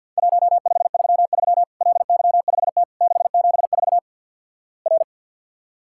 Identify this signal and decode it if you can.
Morse code, heard as a keyed tone, about 50 wpm, 700 Hz. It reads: 1533 LY5T 674 R
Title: Worked All Europe DX Contest - QTC Exchange - 50wpm